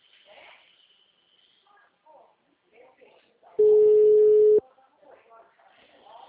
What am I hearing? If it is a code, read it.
Ringback tone (Europe, 425 Hz), with speech.
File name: out- -2011- - - .86699